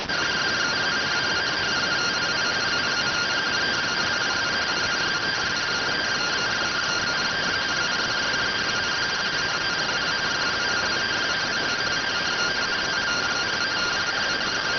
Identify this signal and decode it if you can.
Two-tone teleprinter signal, about 170 Hz shift.